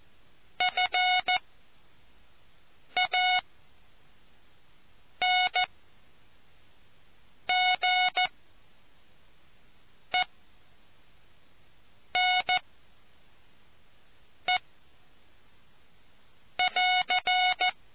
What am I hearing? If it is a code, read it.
Morse code, heard as a keyed tone, about 14 wpm, 730 Hz. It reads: F A N G E N E <AR>